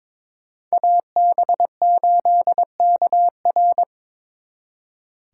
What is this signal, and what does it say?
Morse code, heard as a keyed tone, about 25 wpm, 700 Hz. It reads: AB8KR
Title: Speed Racing - USA 2x2 Call Signs 22wpm